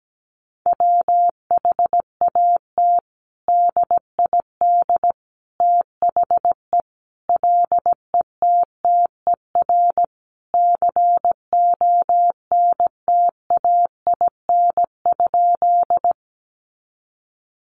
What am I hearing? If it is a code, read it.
Morse code, heard as a keyed tone, about 17 wpm, 700 Hz. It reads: WHAT DID THE LETTER CONTAIN?
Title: Sentences from Top 500 Words 17wpm